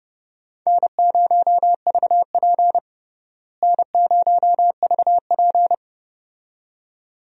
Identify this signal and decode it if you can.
Morse code, heard as a keyed tone, about 30 wpm, 700 Hz. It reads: N0VP N0VP
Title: USA Call Signs - 4 Characters - Repeated 2x - 30wpm